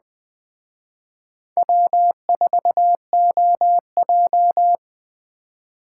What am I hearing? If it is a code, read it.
Morse code, heard as a keyed tone, about 20 wpm, 700 Hz. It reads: W4OJ